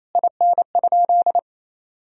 Morse code, heard as a keyed tone, about 28 wpm, 700 Hz. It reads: IN?
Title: Sentences from Top 1000 Words - Review Part 2 - 28wpm